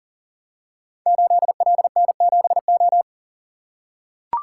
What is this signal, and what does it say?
Morse code, heard as a keyed tone, about 41 wpm, 700 Hz. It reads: 8LN7O
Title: Code Groups - 5 Letter:Numbers - 40wpm